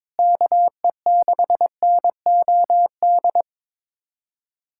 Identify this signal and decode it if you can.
Morse code, heard as a keyed tone, about 22 wpm, 700 Hz. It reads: KE6NOD
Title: USA Callsign DE Callsign - 6 Characters - 22wpm